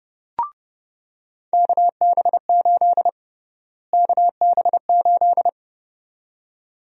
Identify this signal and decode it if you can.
Morse code, heard as a keyed tone, about 32 wpm, 700 Hz. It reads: KB8 KB8
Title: USA Call Sign Prefixes - Repeated 2x - 30wpm